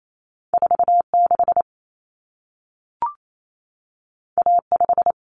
Morse code, heard as a keyed tone, about 28 wpm, 700 Hz. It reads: 46 A5
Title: Common Cut Numbers - 2-Digit 28wpm